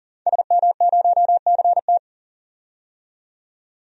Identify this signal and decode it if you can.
Morse code, heard as a keyed tone, about 40 wpm, 700 Hz. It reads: SM0CT
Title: DXCC-100 Call Signs - 5 Characters - 40wpm